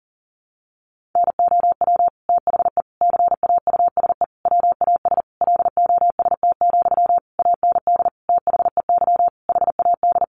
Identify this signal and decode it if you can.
Morse code, heard as a keyed tone, about 40 wpm, 700 Hz. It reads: NOW THE CAUSE WAS LOST, AND THEY HAD
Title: Sentences from Top 1500 Words 40wpm